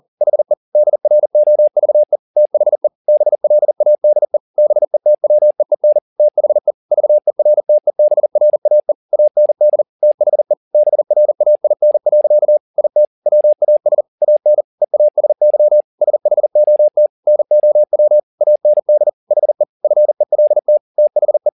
Morse code, heard as a keyed tone, about 40 wpm, 600 Hz. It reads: HE DROVE THE BLADE BETWEEN THE VERTEBRAE AND THE BRAIN. IT WAS AN EASY SHOT NOW AND HE FELT THE
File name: the old man and the sea 2 40wpm